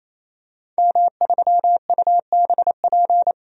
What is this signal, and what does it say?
Morse code, heard as a keyed tone, about 28 wpm, 700 Hz. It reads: M3UBP
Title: DXCC-100 Call Signs - 5 Characters - 28wpm